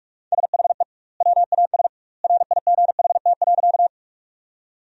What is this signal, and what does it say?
Morse code, heard as a keyed tone, about 44 wpm, 700 Hz. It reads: SHE WAS RIGHT.
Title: 2x Word-Spacing - Sentences from Top 200 Words - 45wpm